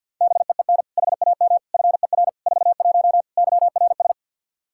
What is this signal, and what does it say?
Morse code, heard as a keyed tone, about 47 wpm, 700 Hz. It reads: BEEN HAM FER 41 YRS
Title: Practice QSO - Exchange 3 - More Info - 50wpm